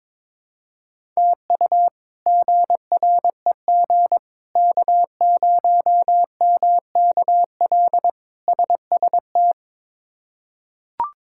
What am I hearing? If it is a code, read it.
Morse code, heard as a keyed tone, about 21 wpm, 700 Hz. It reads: TU GREG K0MKL SST
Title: K1USN SST QSO Practice - 22wpm